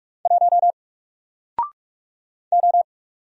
Morse code, heard as a keyed tone, about 45 wpm, 700 Hz. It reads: 1 O
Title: Single Letter-Number Alt 45wpm